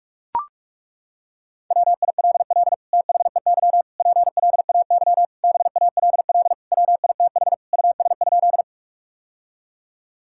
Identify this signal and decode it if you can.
Morse code, heard as a keyed tone, about 41 wpm, 700 Hz. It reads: WILL THEY PLAY BALL WITH US?
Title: Sentences from Top 1000 Words - Review Part 6 - 45wpm